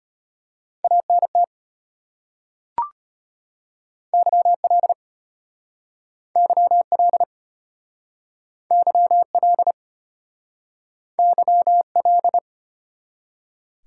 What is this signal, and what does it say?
Morse code, heard as a keyed tone, about 33 wpm, 700 Hz. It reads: ANT YL YL YL YL